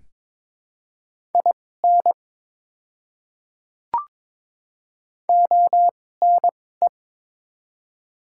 Morse code, heard as a keyed tone, about 22 wpm, spaced as for 16 wpm, 700 Hz. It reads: IN ONE
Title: 2x Character Spacing - Top 200 Words - 22wpm